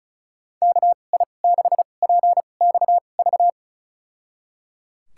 Morse code, heard as a keyed tone, about 36 wpm, spaced as for 25 wpm, 700 Hz. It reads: KI6PXV
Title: USA Call Signs - 6 Characters - 2x Character Spacing - 35wpm